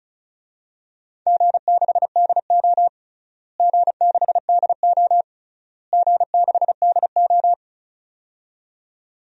Morse code, heard as a keyed tone, about 34 wpm, 700 Hz. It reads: G6DO G6DO G6DO
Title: DXCC-100 Call Signs - 4 Characters - Repeated 3x - 35wpm